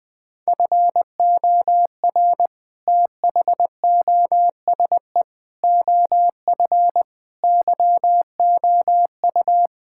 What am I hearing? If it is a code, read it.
Morse code, heard as a keyed tone, about 20 wpm, 700 Hz. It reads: FOR THOSE OF YOU
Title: Sentences with 6-Word Phrases - 20wpm